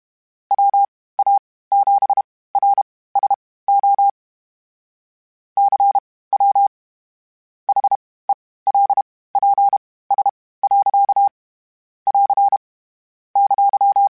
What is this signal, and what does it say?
Morse code, heard as a keyed tone, about 32 wpm, spaced as for 18 wpm, 800 Hz. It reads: WA7RSO CW HELPS. <AR> !